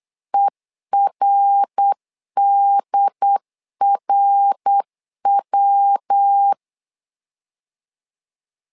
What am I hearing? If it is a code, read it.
Morse code, heard as a keyed tone, about 8 wpm, 800 Hz. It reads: ERDRW
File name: LEZIONI / 3 MWRO / 3 MWR 8wpm 2444